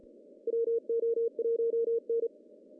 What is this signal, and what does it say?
Morse code, heard as a keyed tone, about 42 wpm, 460 Hz. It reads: WO1N
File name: CW / cqww-cw-k5tr